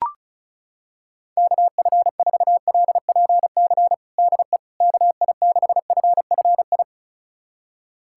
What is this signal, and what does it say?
Morse code, heard as a keyed tone, about 34 wpm, 700 Hz. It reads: KF4LPC DE KI6FFI
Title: USA Callsign DE Callsign - 6 Characters - 35wpm